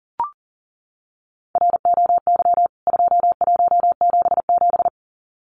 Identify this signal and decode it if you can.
Morse code, heard as a keyed tone, about 37 wpm, 700 Hz. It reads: ROY 2177
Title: CWT Exchange 40wpm